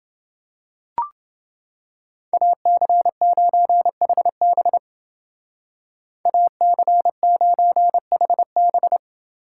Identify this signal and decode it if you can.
Morse code, heard as a keyed tone, about 30 wpm, 700 Hz. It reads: AC9HB AC9HB